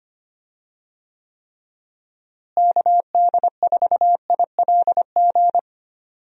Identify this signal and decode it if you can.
Morse code, heard as a keyed tone, about 25 wpm, 700 Hz. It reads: KD4ILG
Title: USA Call Signs - 6 Characters - Repeated 2x - 25wpm